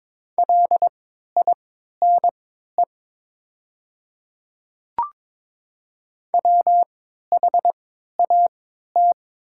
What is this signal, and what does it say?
Morse code, heard as a keyed tone, about 22 wpm, spaced as for 12 wpm, 700 Hz. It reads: LINE WHAT
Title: 3x Character Spacing - Top 200 Words - 22wpm